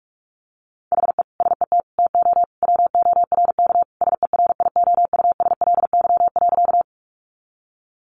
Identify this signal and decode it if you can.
Morse code, heard as a keyed tone, about 44 wpm, 700 Hz. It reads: HE SET TO WORK SERIOUSLY.
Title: Sentences from Top 1200 Words 45wpm